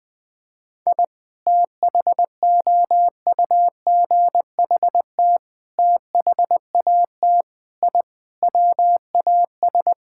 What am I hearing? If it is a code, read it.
Morse code, heard as a keyed tone, about 20 wpm, 700 Hz. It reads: I THOUGHT THAT I WAS
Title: Sentences with 3-Word Phrases - 20wpm